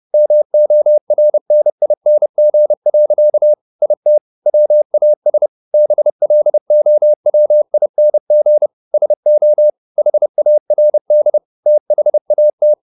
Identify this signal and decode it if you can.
Morse code, heard as a keyed tone, about 30 wpm, 600 Hz. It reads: MORNING. IT WAS BLOWING SO HARD THAT